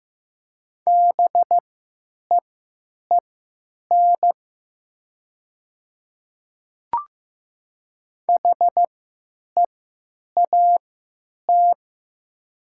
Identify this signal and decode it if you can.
Morse code, heard as a keyed tone, about 15 wpm, spaced as for 8 wpm, 700 Hz. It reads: BEEN HEAT
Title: 3x Character Spacing - Top 500 Words - 15wpm